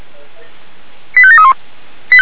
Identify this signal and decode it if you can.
Five-tone selective call (ZVEI), heard as quick runs of five tones, tones of about 70 ms. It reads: 85731